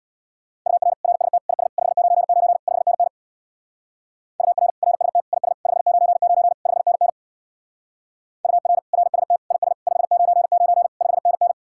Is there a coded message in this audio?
Yes: Morse code from 0.7 to 11.5 s, heard as a keyed tone, about 65 wpm, 700 Hz. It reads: UR RST IS 599 5NN UR RST IS 599 5NN UR RST IS 599 5NN